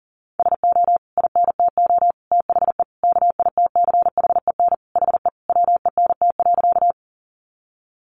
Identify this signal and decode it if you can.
Morse code, heard as a keyed tone, about 40 wpm, 700 Hz. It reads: SO INTO THE KITCHEN HE WENT.
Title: Sentences from Top 1400 Words 40wpm